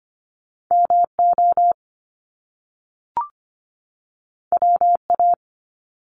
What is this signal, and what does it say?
Morse code, heard as a keyed tone, about 25 wpm, 700 Hz. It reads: MO WA
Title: Top Two Letter Combinations in Words 25wpm